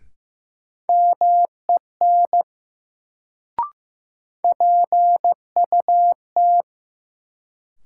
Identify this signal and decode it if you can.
Morse code, heard as a keyed tone, about 15 wpm, 700 Hz. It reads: MEN PUT
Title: Sentences from Top 200 Words - New Word Drill - 15wpm